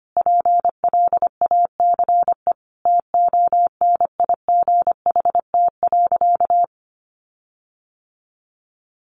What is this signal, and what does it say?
Morse code, heard as a keyed tone, about 25 wpm, 700 Hz. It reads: PLACE TONIGHT.
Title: Sentences from Top 1300 Words 25wpm